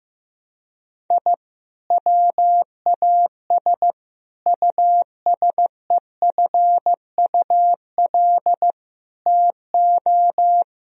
Morse code, heard as a keyed tone, about 15 wpm, 700 Hz. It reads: I WAS USEFUL TO
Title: Sentences from Top 1000 Words - Review Part 4 - 15wpm